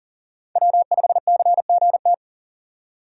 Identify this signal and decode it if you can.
Morse code, heard as a keyed tone, about 40 wpm, 700 Hz. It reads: W5CGT
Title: USA Call Signs - 5 Characters - 40wpm